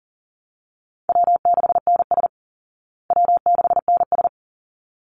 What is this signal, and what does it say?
Morse code, heard as a keyed tone, about 38 wpm, 700 Hz. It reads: W6NS W6NS
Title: Call Signs Repeated 3x - 40wpm